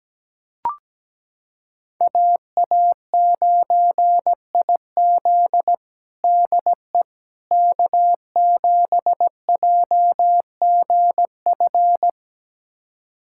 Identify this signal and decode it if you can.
Morse code, heard as a keyed tone, about 17 wpm, 700 Hz. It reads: AA9IZ DE K7JGF
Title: USA Callsign DE Callsign - 5 Characters - 17wpm